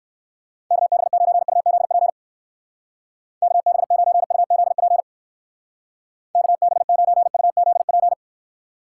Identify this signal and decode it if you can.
Morse code, heard as a keyed tone, about 59 wpm, 700 Hz. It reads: KB9UZP KB9UZP KB9UZP